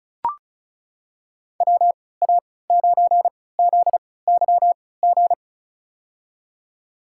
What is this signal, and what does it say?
Morse code, heard as a keyed tone, about 36 wpm, spaced as for 20 wpm, 700 Hz. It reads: WA9ZYG
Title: USA Call Signs - 6 Characters - 3x Character Spacing - 35wpm